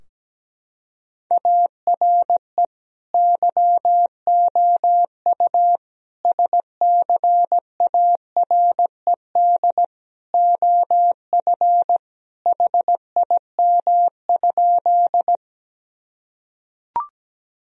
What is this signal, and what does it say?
Morse code, heard as a keyed tone, about 17 wpm, 700 Hz. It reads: ARE YOU SCARED OF HIM?
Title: Sentences from Top 1000 Words - Review Part 4 - 17wpm